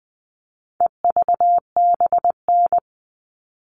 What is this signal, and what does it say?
Morse code, heard as a keyed tone, about 20 wpm, 700 Hz. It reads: EVBN
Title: All Cut Numbers - 4-Digit 20wpm